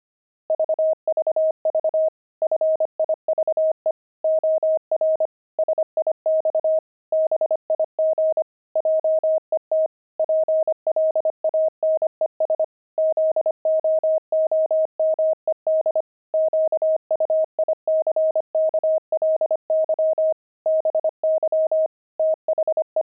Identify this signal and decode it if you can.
Morse code, heard as a keyed tone, about 25 wpm, 630 Hz. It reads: VVV FIVE OR SIX BIG JET PLANES ZOOMED QUICKLY BY THE
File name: pangram-fs-25-cs-25-jet